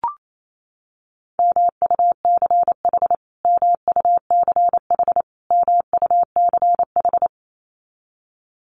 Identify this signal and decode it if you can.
Morse code, heard as a keyed tone, about 28 wpm, 700 Hz. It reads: MUCH MUCH MUCH
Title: Top 500 Words Repeated 28wpm